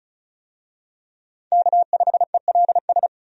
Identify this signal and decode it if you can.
Morse code, heard as a keyed tone, about 36 wpm, 700 Hz. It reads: K5ELS
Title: USA Call Signs - 5 Characters - 35wpm